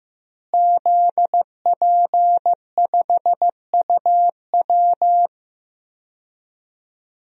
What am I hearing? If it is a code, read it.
Morse code, heard as a keyed tone, about 15 wpm, 700 Hz. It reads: ZP5UW
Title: DXCC-100 Call Signs - 5 Characters - Repeated 2x - 15wpm